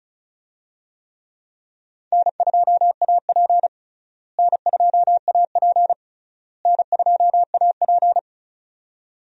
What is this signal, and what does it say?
Morse code, heard as a keyed tone, about 34 wpm, 700 Hz. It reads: N2AP N2AP N2AP
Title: USA Call Signs - 4 Characters - Repeated 3x - 35wpm